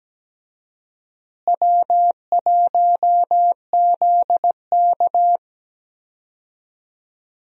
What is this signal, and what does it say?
Morse code, heard as a keyed tone, about 17 wpm, 700 Hz. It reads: W1ZK
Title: USA Call Signs - 4 Characters - Repeated 3x - 17wpm